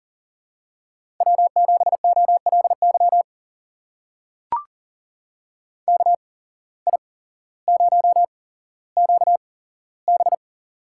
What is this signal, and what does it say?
Morse code, heard as a keyed tone, about 40 wpm, spaced as for 17 wpm, 700 Hz. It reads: W7OLY KI0QB
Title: USA Call Signs - 8x Character Spacing - 40wpm